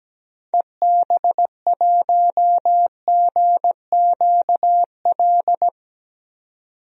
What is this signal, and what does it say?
Morse code, heard as a keyed tone, about 17 wpm, 700 Hz. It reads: EB1GQL
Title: DXCC-100 Call Signs - 6 Characters - 17wpm